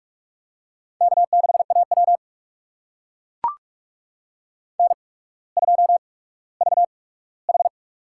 Morse code, heard as a keyed tone, about 44 wpm, spaced as for 19 wpm, 700 Hz. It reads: K6AW N2VH
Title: USA Call Signs - 4 Characters - 8x Character Spacing - 45wpm